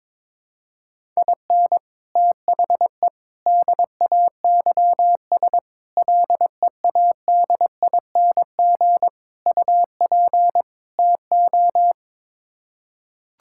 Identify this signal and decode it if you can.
Morse code, heard as a keyed tone, about 22 wpm, 700 Hz. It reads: IN THE DAYS LEADING UP TO
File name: practice / 22wpm / Sets of 6 Words - 22wpm